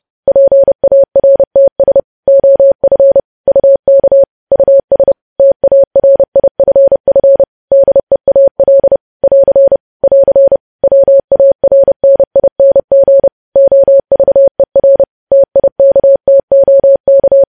Morse code, heard as a keyed tone, about 30 wpm, 550 Hz. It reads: PARTS OF UK US TARIFF DEAL <AR> <AR> WARNING OVER TIKTOK
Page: news headlines in morse code at 30 wpm